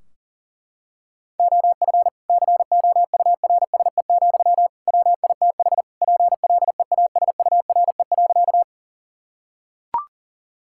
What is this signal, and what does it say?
Morse code, heard as a keyed tone, about 40 wpm, 700 Hz. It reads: OF COURSE, WITH PLEASURE.
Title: Sentences from Top 1600 Words 40wpm